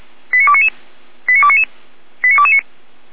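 Five-tone selective call (ZVEI), heard as quick runs of five tones, tones of about 70 ms. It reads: 89299 89299 89229
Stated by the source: ZVEI five-tone selective call received on SDR.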